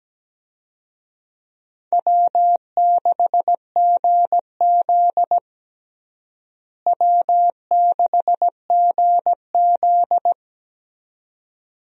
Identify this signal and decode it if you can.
Morse code, heard as a keyed tone, about 17 wpm, 700 Hz. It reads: W6GZ W6GZ
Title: USA Call Signs - 4 Characters - Repeated 2x - 17wpm